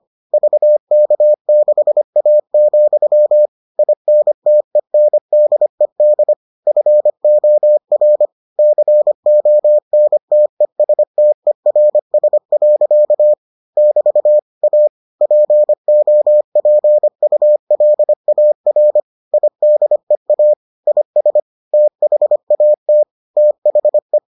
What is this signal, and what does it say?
Morse code, heard as a keyed tone, about 25 wpm, 600 Hz. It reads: VK6A, INTENDED FOR CONTESTERS. <BT> A POPULAR IDEA IS THAT THE